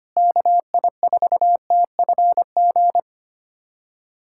Morse code, heard as a keyed tone, about 25 wpm, 700 Hz. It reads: KI4TFG
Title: USA Callsign DE Callsign - 6 Characters - 25wpm